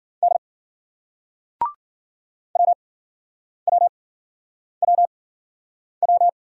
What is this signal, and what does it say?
Morse code, heard as a keyed tone, about 51 wpm, spaced as for 9 wpm, 700 Hz. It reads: D WWWW